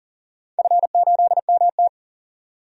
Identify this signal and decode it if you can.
Morse code, heard as a keyed tone, about 40 wpm, 700 Hz. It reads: F8MT
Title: DXCC-100 Call Signs - 4 Characters - 40wpm